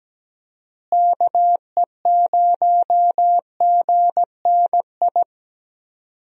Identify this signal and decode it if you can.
Morse code, heard as a keyed tone, about 17 wpm, 700 Hz. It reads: KE0GNI
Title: USA Call Signs - 6 Characters - Repeated 2x - 17wpm